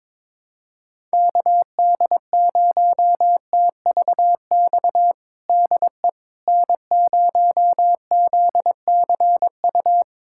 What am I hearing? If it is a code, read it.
Morse code, heard as a keyed tone, about 22 wpm, 700 Hz. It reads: KD0TVX DE N0ZCU
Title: USA Callsign DE Callsign 22wpm